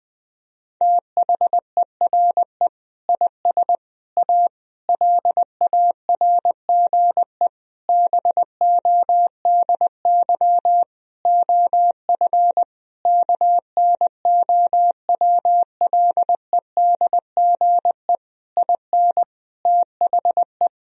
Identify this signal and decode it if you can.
Morse code, heard as a keyed tone, about 21 wpm, 700 Hz. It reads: THERE IS A LARGE BODY OF KNOWLEDGE IN THE
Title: Sentences from Top 1000 Words - Review Part 4 - 20wpm